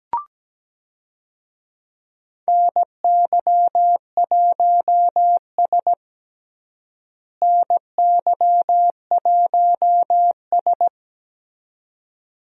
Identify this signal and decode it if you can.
Morse code, heard as a keyed tone, about 17 wpm, 700 Hz. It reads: NY1S NY1S
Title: USA Call Signs - 4 Characters - Repeated 2x - 17wpm